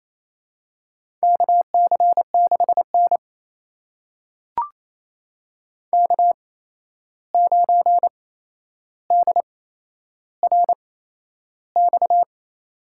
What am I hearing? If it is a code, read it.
Morse code, heard as a keyed tone, about 28 wpm, spaced as for 11 wpm, 700 Hz. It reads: KC6N K9DRX
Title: USA Call Signs - 8x Character Spacing - 28wpm